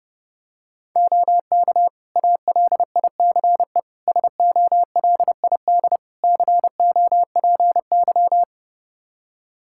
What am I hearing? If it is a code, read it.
Morse code, heard as a keyed tone, about 30 wpm, 700 Hz. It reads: OK ALICE SOLID COPY
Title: Practice QSO - Exchange 3 - More Info - 30wpm